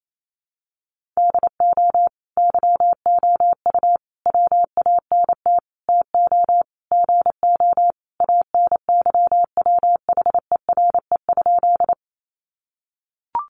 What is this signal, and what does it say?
Morse code, heard as a keyed tone, about 28 wpm, 700 Hz. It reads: DO YOU WANT TO GO ANYWHERE?
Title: Sentences from Top 1400 Words 28wpm